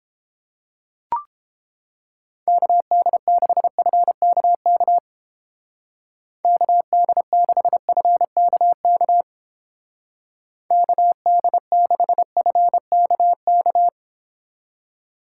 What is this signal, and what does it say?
Morse code, heard as a keyed tone, about 29 wpm, 700 Hz. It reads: KD6FKK KD6FKK KD6FKK